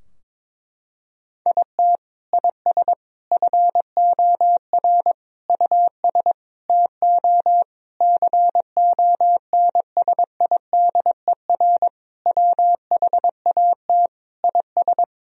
Morse code, heard as a keyed tone, about 22 wpm, 700 Hz. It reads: IT IS FOR US TO CONSIDER WHAT IS
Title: Sentences from Top 1000 Words - Review Part 4 - 22wpm